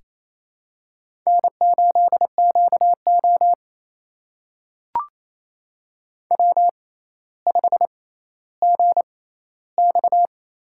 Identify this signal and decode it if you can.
Morse code, heard as a keyed tone, about 28 wpm, spaced as for 14 wpm, 700 Hz. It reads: N8QO W5GX
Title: USA Call Signs - 4 Characters - 6x Character Spacing - 28wpm